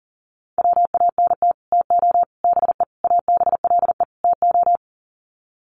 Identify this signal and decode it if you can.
Morse code, heard as a keyed tone, about 40 wpm, 700 Hz. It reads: WANT TO BE ABLE TO
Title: Sets of 5 Words 40wpm